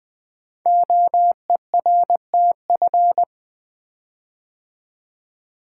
Morse code, heard as a keyed tone, about 20 wpm, 700 Hz. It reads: OERTF
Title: Code Groups - 5 Letters - 20wpm